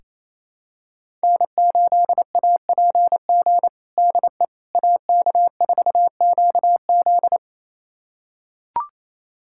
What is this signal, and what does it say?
Morse code, heard as a keyed tone, about 28 wpm, 700 Hz. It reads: N8APG DE AK4QZ
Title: USA Callsign DE Callsign - 5 Characters - 28wpm